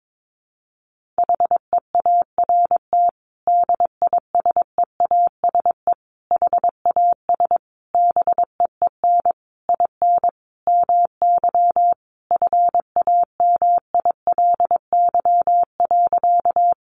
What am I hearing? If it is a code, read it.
Morse code, heard as a keyed tone, about 22 wpm, 700 Hz. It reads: HEART DISEASE HAS BEEN IN MY FAMILY.
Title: Sentences from Top 1200 Words 22wpm